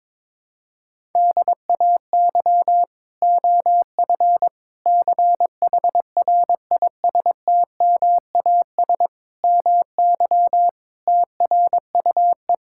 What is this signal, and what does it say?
Morse code, heard as a keyed tone, about 22 wpm, 700 Hz. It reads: DAY OF CHRISTMAS MY TRUE
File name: practice / 22wpm / Sets of 8 Words - 22wpm